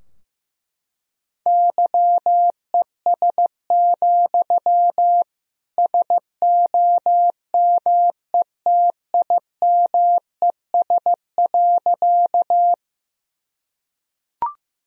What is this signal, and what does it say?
Morse code, heard as a keyed tone, about 15 wpm, 700 Hz. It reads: YES, SOMETIMES.
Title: Sentences from Top 1000 Words - Review Part 3 - 15wpm